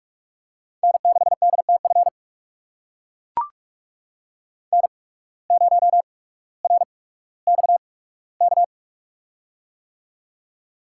Morse code, heard as a keyed tone, about 44 wpm, spaced as for 19 wpm, 700 Hz. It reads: N6DTF N0RXK
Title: USA Call Signs - 8x Character Spacing - 45wpm